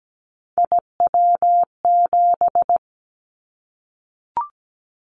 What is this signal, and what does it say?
Morse code, heard as a keyed tone, about 17 wpm, 700 Hz. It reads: IW7
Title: Callsign Prefixes for Top 25 DXCC - 17wpm